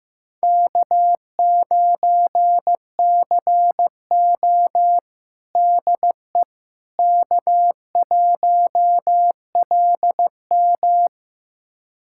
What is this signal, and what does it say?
Morse code, heard as a keyed tone, about 15 wpm, 700 Hz. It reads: K9CO DE K1LM
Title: USA Callsign DE Callsign - 4 Characters - 15wpm